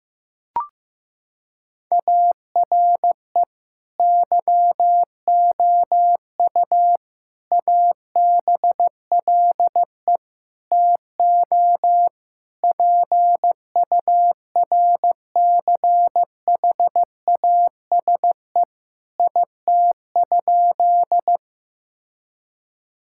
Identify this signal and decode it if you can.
Morse code, heard as a keyed tone, about 15 wpm, 700 Hz. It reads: ARE YOU ABLE TO PURCHASE IT?
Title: Sentences from Top 1000 Words - Review Part 5 - 15wpm